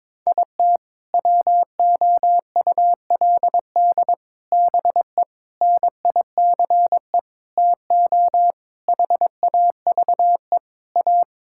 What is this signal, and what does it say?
Morse code, heard as a keyed tone, about 21 wpm, 700 Hz. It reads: IT WOULD BE NICE TO HAVE A
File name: practice / 22wpm / Sets of 7 Words - 22wpm